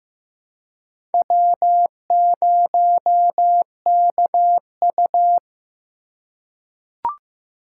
Morse code, heard as a keyed tone, about 15 wpm, 700 Hz. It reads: W0KU
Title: USA Call Signs - 4 Characters - 2x Character Spacing - 15wpm